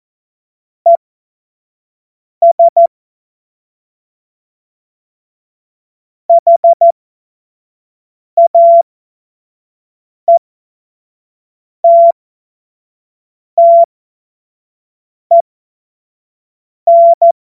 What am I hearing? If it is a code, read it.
Morse code, heard as a keyed tone, about 14 wpm, spaced as for 5 wpm, 680 Hz. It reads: ES HAETTEN
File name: Morsekurs / Witze / Witz 31. Mai 2022 25 F70